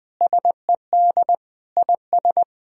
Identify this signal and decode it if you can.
Morse code, heard as a keyed tone, about 20 wpm, 700 Hz. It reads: SED IS
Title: Sentences from Top 200 Words and Abbreviations 20wpm